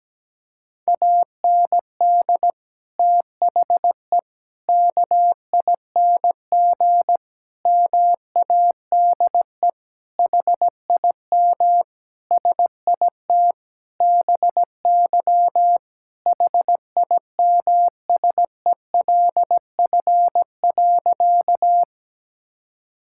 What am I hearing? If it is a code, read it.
Morse code, heard as a keyed tone, about 17 wpm, 700 Hz. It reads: AND THE KING MADE HIM SIT BY HIMSELF.
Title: Sentences from Top 1600 Words 17wpm